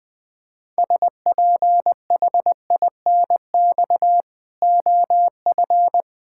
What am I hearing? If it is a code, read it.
Morse code, heard as a keyed tone, about 20 wpm, 700 Hz. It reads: SPHINX OF
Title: Pangram Challenge v2 - 20wpm